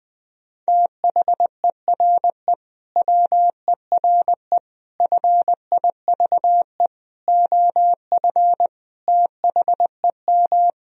Morse code, heard as a keyed tone, about 20 wpm, 700 Hz. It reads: THERE WERE FIVE OF THEM
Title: Sentences from Top 1000 Words - Review Part 2 - 20wpm